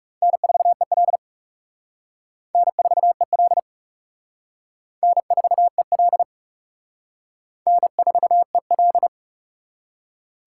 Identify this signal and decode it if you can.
Morse code, heard as a keyed tone, about 37 wpm, 700 Hz. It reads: N4EL N4EL N4EL N4EL
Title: Speed Racing - USA Call Signs 30wpm